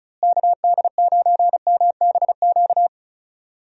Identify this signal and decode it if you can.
Morse code, heard as a keyed tone, about 34 wpm, 700 Hz. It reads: KD9MBQ
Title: USA Call Signs - 6 Characters - Repeated 2x - 35wpm